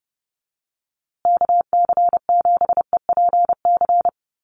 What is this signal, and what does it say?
Morse code, heard as a keyed tone, about 30 wpm, 700 Hz. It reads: KC7EPC
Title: USA Call Signs Repeated 3x 30wpm